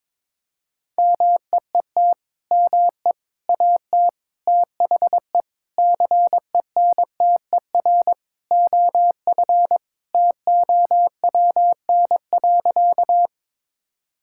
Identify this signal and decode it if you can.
Morse code, heard as a keyed tone, about 22 wpm, 700 Hz. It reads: MEET ME AT THE CENTER OF TOWN.
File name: practice / 22wpm / Sentences with 3-Word Phrases - 22wpm